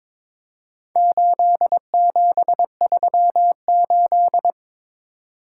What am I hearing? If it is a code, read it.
Morse code, heard as a keyed tone, about 22 wpm, 690 Hz. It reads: 8738
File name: practice / 22wpm / Four Letter-Number Combinations - 22wpm